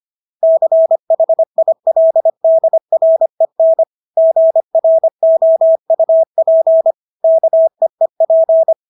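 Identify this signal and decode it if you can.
Morse code, heard as a keyed tone, about 25 wpm, 650 Hz. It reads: CHILDREN GROUP KEEP
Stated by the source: Morse practice of English words at 25 wpm from CW med Gnister.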